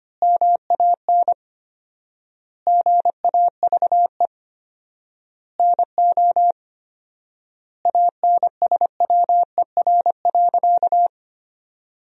Morse code, heard as a keyed tone, about 25 wpm, 700 Hz. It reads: MAN GAVE NO ANSWER.
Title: 4x Word-Spacing - Sentences from Top 500 Words - 25wpm